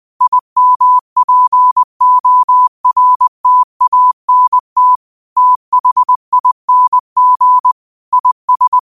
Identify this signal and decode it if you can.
Morse code, heard as a keyed tone, about 20 wpm, 1000 Hz. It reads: IMPORTANT THING IS
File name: qotdfiles / 20wpm